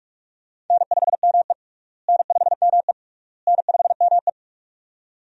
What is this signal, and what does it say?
Morse code, heard as a keyed tone, about 46 wpm, 700 Hz. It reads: N5ME N5ME N5ME
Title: USA Call Signs - 4 Characters - Repeated 3x - 45wpm